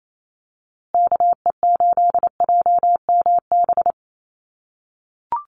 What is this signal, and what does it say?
Morse code, heard as a keyed tone, about 28 wpm, 700 Hz. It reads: KE8JMB
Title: USA Call Signs Repeated 3x 28wpm